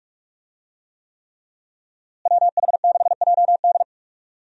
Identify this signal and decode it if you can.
Morse code, heard as a keyed tone, about 43 wpm, 700 Hz. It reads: WH6JD
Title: USA Call Signs - 5 Characters - Repeated 3x - 45wpm